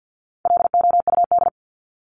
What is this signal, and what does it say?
Morse code, heard as a keyed tone, about 50 wpm, 700 Hz. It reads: LOUD
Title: Top 1751-2000 Words 50wpm